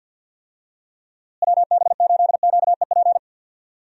Morse code, heard as a keyed tone, about 51 wpm, 700 Hz. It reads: WB8QEP
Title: USA Call Signs - 6 Characters - Repeated 2x - 50wpm